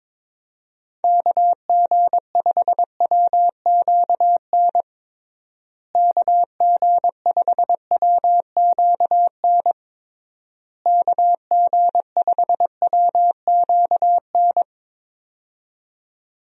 Morse code, heard as a keyed tone, about 22 wpm, 700 Hz. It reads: KG5WQN KG5WQN KG5WQN
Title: USA Call Signs - 6 Characters - Repeated 3x - 22wpm